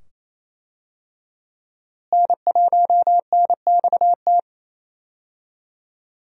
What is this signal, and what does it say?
Morse code, heard as a keyed tone, about 27 wpm, 700 Hz. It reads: N1NXT
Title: USA Call Signs - 5 Characters - Repeated 2x - 28wpm